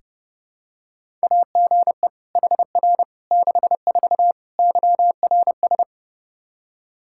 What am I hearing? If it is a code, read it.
Morse code, heard as a keyed tone, about 32 wpm, 700 Hz. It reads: AGE HR 64 YRS
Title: Practice QSO - Exchange 3 - More Info - 30wpm